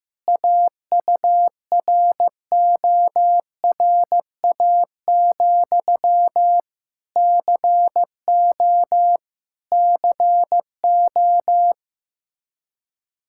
Morse code, heard as a keyed tone, about 16 wpm, 700 Hz. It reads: AURORA, CO CO
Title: Practice QSO - Exchange 1 - Beginning - 15wpm